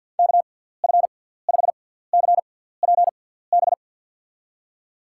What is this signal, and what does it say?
Morse code, heard as a keyed tone, about 49 wpm, spaced as for 17 wpm, 700 Hz. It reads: KF5CPB
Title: USA Call Signs - 6 Characters - 6x Character Spacing - 50wpm